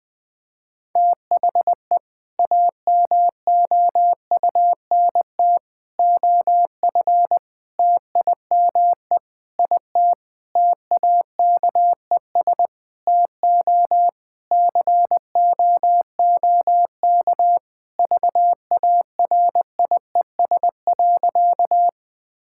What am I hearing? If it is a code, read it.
Morse code, heard as a keyed tone, about 20 wpm, 700 Hz. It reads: THE AMOUNT OF TIME IT TAKES TO COOK VARIES.
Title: Sentences with 7-Word Phrases - 20wpm